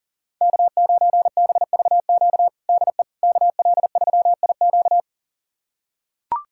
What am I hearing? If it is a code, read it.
Morse code, heard as a keyed tone, about 37 wpm, 700 Hz. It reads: K9BVQ DE KL3IQ
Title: USA Callsign DE Callsign - 5 Characters - 40wpm